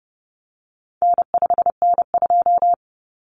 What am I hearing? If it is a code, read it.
Morse code, heard as a keyed tone, about 30 wpm, 700 Hz. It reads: N5N2
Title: Common Cut Numbers - 4-Digit 30wpm